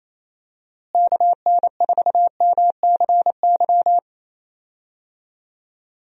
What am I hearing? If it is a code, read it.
Morse code, heard as a keyed tone, about 28 wpm, 700 Hz. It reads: KN4MCY
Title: USA Call Signs - 6 Characters - Repeated 2x - 28wpm